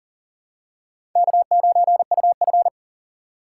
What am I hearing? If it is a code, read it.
Morse code, heard as a keyed tone, about 38 wpm, 700 Hz. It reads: K9UF
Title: USA Call Signs - 3x Character Spacing - 40wpm